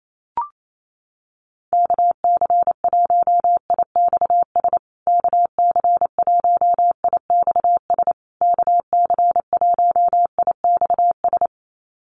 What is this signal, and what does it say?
Morse code, heard as a keyed tone, about 28 wpm, 700 Hz. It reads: KC1IXS KC1IXS KC1IXS
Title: USA Call Signs Repeated 3x 28wpm